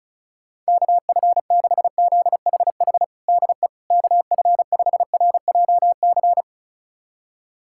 Morse code, heard as a keyed tone, about 36 wpm, 700 Hz. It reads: KF6ZHH DE KF5RJC
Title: USA Callsign DE Callsign 35wpm